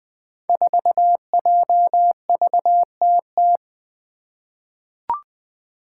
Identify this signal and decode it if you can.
Morse code, heard as a keyed tone, about 20 wpm, 700 Hz. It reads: 4JVTT
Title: Five Challenging Letter-Number Combinations 20wpm